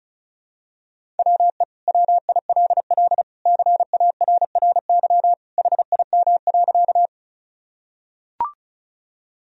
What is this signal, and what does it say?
Morse code, heard as a keyed tone, about 34 wpm, 700 Hz. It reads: WE WILL CARRY HIM.
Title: Sentences from Top 1000 Words - Review Part 1 - 35wpm